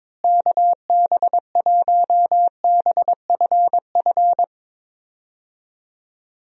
Morse code, heard as a keyed tone, about 22 wpm, 700 Hz. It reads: KB1BFF
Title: USA Call Signs - 6 Characters - Repeated 3x - 22wpm